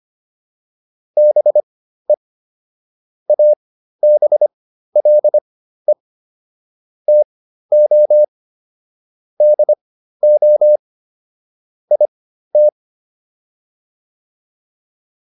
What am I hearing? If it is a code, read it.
Morse code, heard as a keyed tone, about 25 wpm, spaced as for 13 wpm, 600 Hz. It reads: BE ABLE TO DO IT